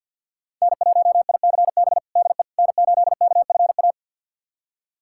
Morse code, heard as a keyed tone, about 51 wpm, 700 Hz. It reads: N1ICB DE N8KFA
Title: USA Callsign DE Callsign - 5 Characters - 50wpm